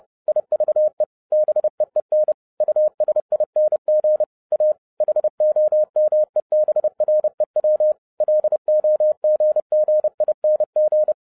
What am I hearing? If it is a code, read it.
Morse code, heard as a keyed tone, about 30 wpm, 610 Hz. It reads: IVE BEEN USING A HOMEBREW LOGGING
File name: 30wpm